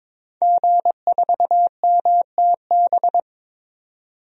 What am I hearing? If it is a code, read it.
Morse code, heard as a keyed tone, about 21 wpm, 700 Hz. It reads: G4MTB
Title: DXCC-100 Call Signs - 5 Characters - 22wpm